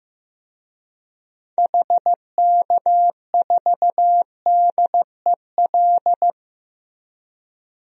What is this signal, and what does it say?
Morse code, heard as a keyed tone, about 15 wpm, 700 Hz. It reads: HK4DEL
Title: DXCC-100 Call Signs - 6 Characters - 15wpm